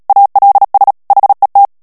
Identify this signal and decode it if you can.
Morse code, heard as a keyed tone, about 37 wpm, 800 Hz. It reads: ALS HET
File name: morse mp3 / HOLLANDAIS texte clair 2 vitesse 35 mots minute